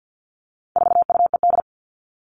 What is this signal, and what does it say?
Morse code, heard as a keyed tone, about 50 wpm, 700 Hz. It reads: 4UED
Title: All Cut Numbers - 4-Digit 50wpm